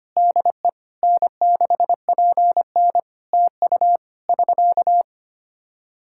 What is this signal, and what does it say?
Morse code, heard as a keyed tone, about 25 wpm, 700 Hz. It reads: DE N6PN TU <SK>
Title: K1USN SST QSO Practice 25wpm